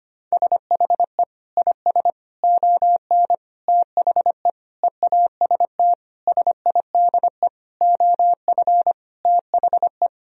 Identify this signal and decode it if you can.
Morse code, heard as a keyed tone, about 25 wpm, 700 Hz. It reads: SHE IS ON THE EAST SIDE OF THE
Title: Sentences from Top 1000 Words - Review Part 3 - 25wpm